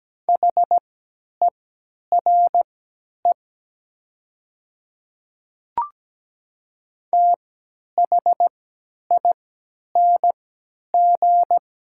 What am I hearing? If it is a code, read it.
Morse code, heard as a keyed tone, about 17 wpm, spaced as for 10 wpm, 700 Hz. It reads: HERE THING
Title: 3x Character Spacing - Top 300 Words - 17wpm